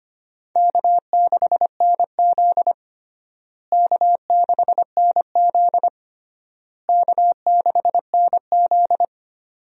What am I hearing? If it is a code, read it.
Morse code, heard as a keyed tone, about 25 wpm, 700 Hz. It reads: K6NZ K6NZ K6NZ
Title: USA Call Signs - 4 Characters - Repeated 3x - 25wpm